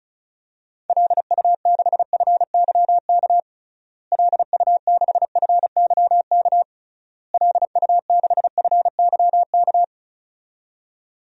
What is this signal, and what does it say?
Morse code, heard as a keyed tone, about 36 wpm, 700 Hz. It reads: LU6FYK LU6FYK LU6FYK
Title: DXCC-100 Call Signs - 6 Characters - Repeated 3x - 35wpm